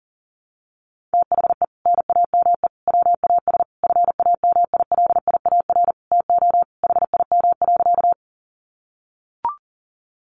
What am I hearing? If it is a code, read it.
Morse code, heard as a keyed tone, about 40 wpm, 700 Hz. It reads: THE NAME WAS FAMILIAR TO HIM.
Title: Sentences from Top 1000 Words 40wpm